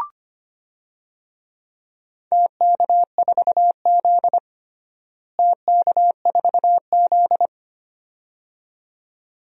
Morse code, heard as a keyed tone, about 25 wpm, 700 Hz. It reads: TK4Z TK4Z
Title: DXCC-100 Call Signs - 4 Characters - Repeated 2x - 25wpm